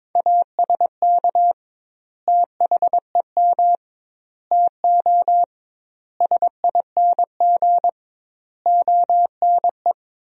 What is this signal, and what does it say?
Morse code, heard as a keyed tone, about 22 wpm, 700 Hz. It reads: ASK THEM TO SING ONE
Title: 2x Word-Spacing - Sentences from Top 400 Words - 22wpm